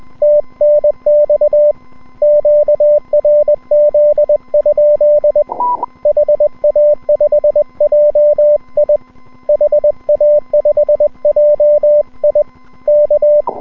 Morse code, heard as a keyed tone, about 21 wpm, 590 Hz. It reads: TNX QRZ? HA5JI HA5JI K
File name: n6tr / N6TR-7deHA5JIon160M-06-Sep- 31UTC